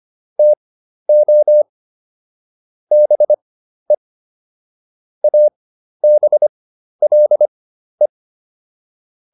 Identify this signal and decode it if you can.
Morse code, heard as a keyed tone, about 25 wpm, spaced as for 12 wpm, 600 Hz. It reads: TO BE ABLE